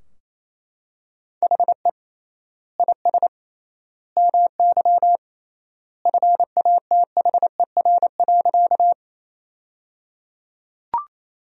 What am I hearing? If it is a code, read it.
Morse code, heard as a keyed tone, about 28 wpm, 700 Hz. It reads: HE IS MY FATHER.
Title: 3x Word-Spacing - Sentences from Top 300 Words - 28wpm